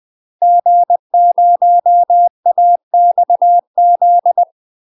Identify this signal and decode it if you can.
Morse code, heard as a keyed tone, about 20 wpm, 700 Hz. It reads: G0AXZ
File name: UK MASTER QSOs TRAINING / 20WPM / UK MASTER QSOs 0036